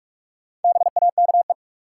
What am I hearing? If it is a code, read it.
Morse code, heard as a keyed tone, about 46 wpm, 700 Hz. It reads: BAKE
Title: Four Letter Words - 45wpm